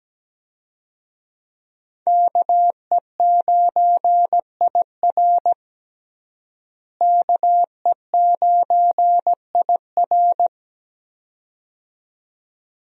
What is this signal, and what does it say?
Morse code, heard as a keyed tone, about 17 wpm, 700 Hz. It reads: KE9IR KE9IR
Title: USA Call Signs - 5 Characters - Repeated 2x - 17wpm